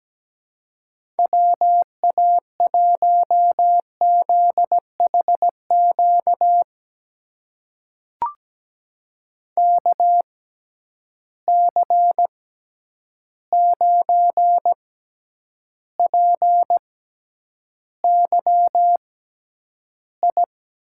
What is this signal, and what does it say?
Morse code, heard as a keyed tone, about 17 wpm, spaced as for 9 wpm, 700 Hz. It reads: WA1ZHQ KC9PYI
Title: USA Call Signs - 6 Characters - 6x Character Spacing - 17wpm